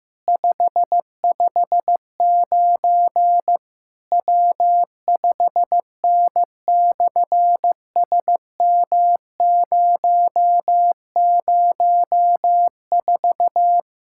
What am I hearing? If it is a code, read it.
Morse code, heard as a keyed tone, about 14 wpm, 700 Hz. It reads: 559 W5N/SM004
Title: SOTA QSO Practice 15wpm